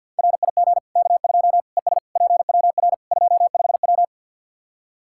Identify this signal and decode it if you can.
Morse code, heard as a keyed tone, about 48 wpm, 700 Hz. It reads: RIG K2 ES PWR 25W
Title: Practice QSO - Exchange 2 - Info - 50wpm